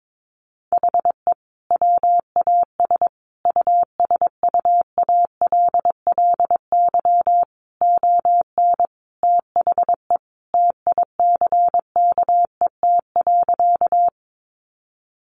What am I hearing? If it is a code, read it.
Morse code, heard as a keyed tone, about 22 wpm, 700 Hz. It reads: HE WAS USUALLY ON THE TICKET.
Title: Sentences from Top 1400 Words 22wpm